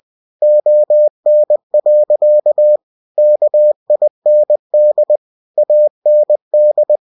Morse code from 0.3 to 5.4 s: ON. KIND